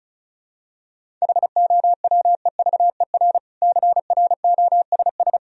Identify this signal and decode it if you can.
Morse code, heard as a keyed tone, about 34 wpm, 700 Hz. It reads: HOWEVER CROSS
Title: Sentences from Top 1000 Words - Review Part 1 - 35wpm